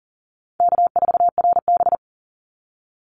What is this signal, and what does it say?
Morse code, heard as a keyed tone, about 40 wpm, 700 Hz. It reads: K4RB
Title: Easy Call Signs - 40wpm